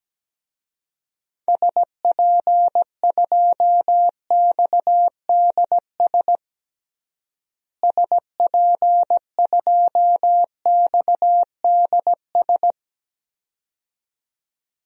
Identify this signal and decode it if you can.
Morse code, heard as a keyed tone, about 17 wpm, 700 Hz. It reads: SP2XDS SP2XDS
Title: DXCC-100 Call Signs - 6 Characters - Repeated 2x - 17wpm